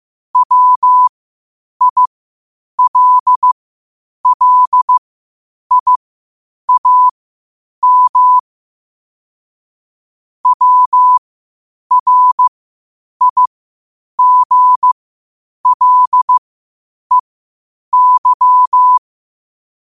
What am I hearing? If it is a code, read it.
Morse code, heard as a keyed tone, about 15 wpm, spaced as for 8 wpm, 1000 Hz. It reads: WILLIAM WRIGLEY
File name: qotdfiles / 8wpm